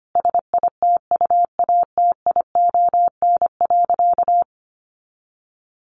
Morse code, heard as a keyed tone, about 25 wpm, 700 Hz. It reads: SITUATION.
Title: Sentences from Top 1200 Words 25wpm